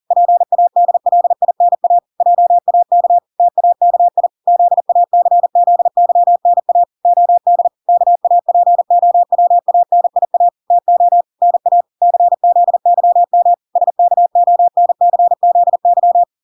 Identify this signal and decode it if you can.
Morse code, heard as a keyed tone, about 40 wpm, 700 Hz. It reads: PADLINA JAK TAKI ZACZYNA OD KAPOWANIA TO NA CZYM SKONCZY